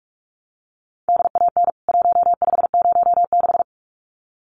Morse code, heard as a keyed tone, about 44 wpm, 700 Hz. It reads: DAN 1506
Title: CWT Exchange 45wpm